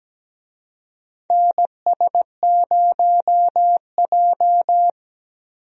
Morse code, heard as a keyed tone, about 17 wpm, 700 Hz. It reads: NS0J
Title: USA Call Signs - 4 Characters - Repeated 2x - 17wpm